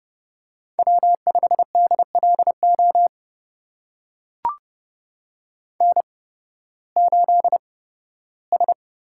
Morse code, heard as a keyed tone, about 32 wpm, spaced as for 16 wpm, 700 Hz. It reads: W5DLO N8S
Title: USA Call Signs - 5 Characters - 8x Character Spacing - 30wpm